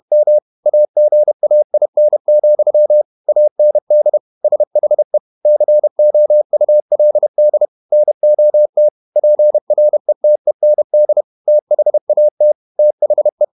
Morse code, heard as a keyed tone, about 32 wpm, 600 Hz. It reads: M AGAIN, AND SHE COULD NOT PRETEND THAT THE
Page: Never 31wpm | CW med Gnister